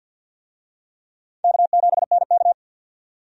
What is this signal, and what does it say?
Morse code, heard as a keyed tone, about 51 wpm, 700 Hz. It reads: K7NK
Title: USA Call Signs - 4 Characters - 50wpm